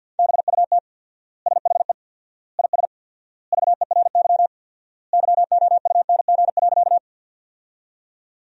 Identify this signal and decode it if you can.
Morse code, heard as a keyed tone, about 50 wpm, 700 Hz. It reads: BUT SHE IS VERY YOUNG.
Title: 4x Word-Spacing - Sentences from Top 400 Words - 50wpm